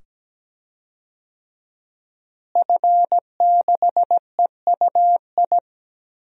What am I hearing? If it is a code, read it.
Morse code, heard as a keyed tone, about 17 wpm, 700 Hz. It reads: F6EUI
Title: DXCC-100 Call Signs - 5 Characters - Repeated 2x - 17wpm